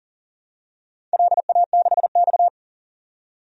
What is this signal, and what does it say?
Morse code, heard as a keyed tone, about 40 wpm, 700 Hz. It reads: LA6X
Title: DXCC-100 Call Signs - 4 Characters - 40wpm